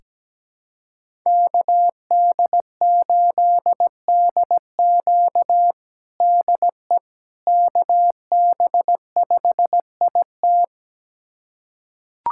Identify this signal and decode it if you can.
Morse code, heard as a keyed tone, about 17 wpm, 700 Hz. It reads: KD8DQ DE KB5IT
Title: USA Callsign DE Callsign - 5 Characters - 17wpm